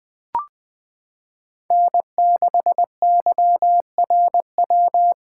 Morse code, heard as a keyed tone, about 20 wpm, 700 Hz. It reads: N6YRW
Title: USA Callsign DE Callsign - 5 Characters - 20wpm